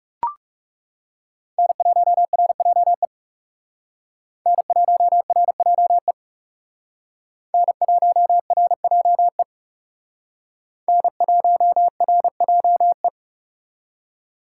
Morse code, heard as a keyed tone, about 39 wpm, 700 Hz. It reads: N1RJE N1RJE N1RJE N1RJE